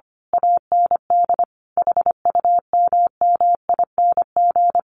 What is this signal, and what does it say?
Morse code, heard as a keyed tone, about 25 wpm, 700 Hz. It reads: AND HUMMING
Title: A Sound Of Thunder 25wpm